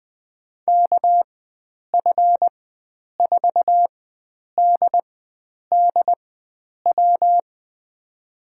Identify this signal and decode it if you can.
Morse code, heard as a keyed tone, about 20 wpm, spaced as for 9 wpm, 700 Hz. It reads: KF4DDW
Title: USA Call Signs - 6 Characters - 4x Character Spacing - 20wpm